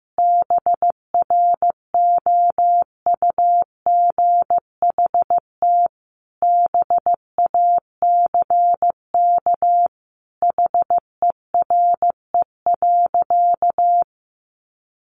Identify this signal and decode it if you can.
Morse code, heard as a keyed tone, about 15 wpm, 700 Hz. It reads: BROUGHT BACK HERE.
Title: Sentences from Top 600 Words 15wpm